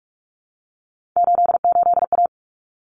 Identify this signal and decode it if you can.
Morse code, heard as a keyed tone, about 44 wpm, 700 Hz. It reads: 88A
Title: Common Cut Numbers - 3-Digit 45wpm